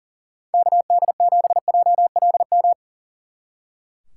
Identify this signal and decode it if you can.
Morse code, heard as a keyed tone, about 48 wpm, 700 Hz. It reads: KD7JLM
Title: Speed Racing - USA Call Signs 40wpm